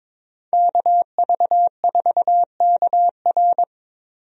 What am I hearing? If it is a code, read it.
Morse code, heard as a keyed tone, about 22 wpm, 700 Hz. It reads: KV4KR
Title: USA Call Signs - 5 Characters - Repeated 3x - 22wpm